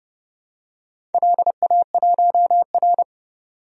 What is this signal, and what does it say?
Morse code, heard as a keyed tone, about 31 wpm, 700 Hz. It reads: LA1R
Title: DXCC-100 Call Signs - 4 Characters - 30wpm